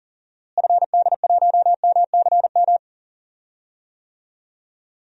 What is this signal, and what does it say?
Morse code, heard as a keyed tone, about 40 wpm, 700 Hz. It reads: FD1MCM
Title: DXCC-100 Call Signs - 6 Characters - Repeated 3x - 40wpm